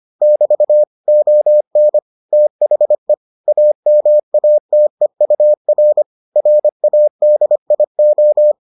Morse code, heard as a keyed tone, about 25 wpm, 600 Hz. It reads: <BT> ON THE AMATEUR RADIO